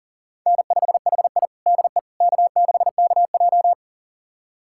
Morse code, heard as a keyed tone, about 40 wpm, 700 Hz. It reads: N5HI DE K6KJ
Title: USA Callsign DE Callsign - 4 Characters - 40wpm